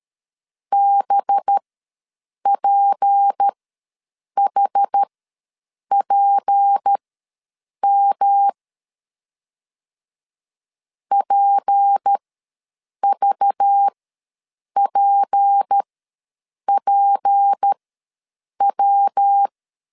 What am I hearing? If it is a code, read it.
Morse code, heard as a keyed tone, about 13 wpm, spaced as for 7 wpm, 800 Hz. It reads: BPHPM PVPPW
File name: LEZIONI / 6 PKX0 / 6 P ws 12wpm 3117